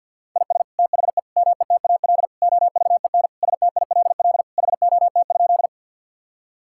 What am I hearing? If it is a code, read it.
Morse code, heard as a keyed tone, about 50 wpm, 700 Hz. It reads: IS THE METAL OVEN STILL HOT?
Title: Sentences from Top 1000 Words - Review Part 6 - 50wpm